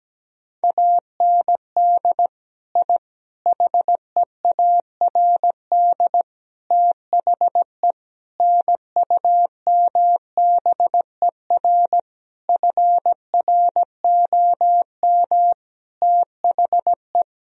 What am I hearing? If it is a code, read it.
Morse code, heard as a keyed tone, about 17 wpm, 700 Hz. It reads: AND I HEARD THE NUMBER FROM THE
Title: Sentences from Top 1000 Words - Review Part 1 - 17wpm